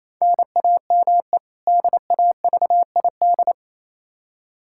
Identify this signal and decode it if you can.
Morse code, heard as a keyed tone, about 28 wpm, 700 Hz. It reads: NAME DAVID
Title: K1USN SST QSO Practice 28wpm